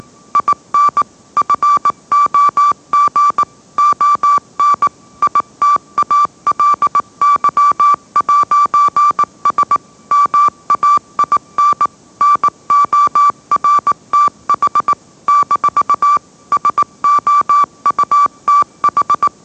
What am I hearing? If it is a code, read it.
Morse code, heard as a keyed tone, about 18 wpm, 1160 Hz. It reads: IN FOG ON ITALY'S MAIN NORTH - SOUTH
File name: mrx / mrx082 20wpm